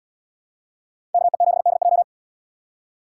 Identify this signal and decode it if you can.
Morse code, heard as a keyed tone, about 71 wpm, 690 Hz. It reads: KE7DJ